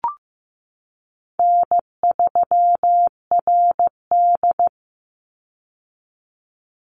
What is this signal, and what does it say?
Morse code, heard as a keyed tone, about 15 wpm, 700 Hz. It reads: N3RD
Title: Easy Call Signs - 15wpm